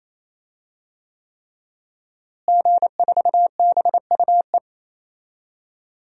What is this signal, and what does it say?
Morse code, heard as a keyed tone, about 28 wpm, 700 Hz. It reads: G4BUE
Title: DXCC-100 Call Signs - 5 Characters - Repeated 3x - 28wpm